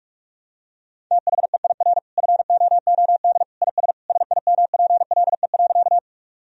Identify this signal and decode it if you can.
Morse code, heard as a keyed tone, about 41 wpm, 700 Hz. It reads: THEIR FOOD IS SIMPLE.
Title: Sentences from Top 1000 Words - Review Part 1 - 45wpm